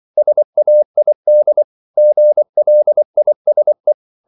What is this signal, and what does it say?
Morse code, heard as a keyed tone, about 24 wpm, 600 Hz. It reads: SAID GLISE